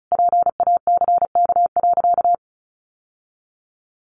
Morse code, heard as a keyed tone, about 34 wpm, 700 Hz. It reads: PACK.
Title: Sentences from Top 1300 Words 35wpm